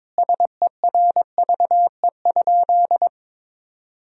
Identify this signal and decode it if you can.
Morse code, heard as a keyed tone, about 23 wpm, 700 Hz. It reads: SERVE?
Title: 4x Word-Spacing - Sentences from Top 400 Words - 22wpm